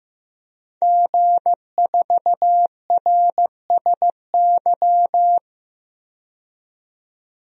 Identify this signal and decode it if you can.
Morse code, heard as a keyed tone, about 15 wpm, 700 Hz. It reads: G4RSY
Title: DXCC-100 Call Signs - 5 Characters - 15wpm